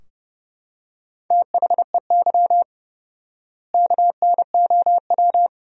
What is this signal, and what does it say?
Morse code, heard as a keyed tone, about 30 wpm, 700 Hz. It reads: THEY KNOW
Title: 4x Word-Spacing - Sentences from Top 500 Words - 30wpm